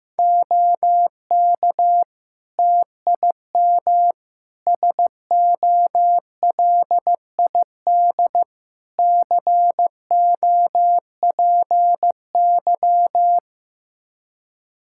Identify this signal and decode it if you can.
Morse code, heard as a keyed tone, about 15 wpm, 700 Hz. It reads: OK TIM SOLID COPY
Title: Practice QSO - Exchange 3 - More Info - 15wpm